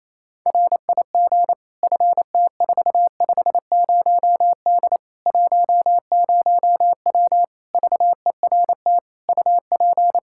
Morse code, heard as a keyed tone, about 28 wpm, 700 Hz. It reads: RIG FT450D 10W VERT UP
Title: Practice QSO - Exchange 2 - Info - 28wpm